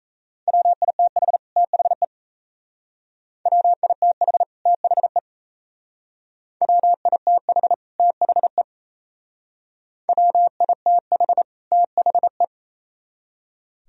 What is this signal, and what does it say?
Morse code, heard as a keyed tone, about 35 wpm, 700 Hz. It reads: WITH THE WITH THE WITH THE WITH THE
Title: Speed-Racing - Sets of 2 Words 28wpm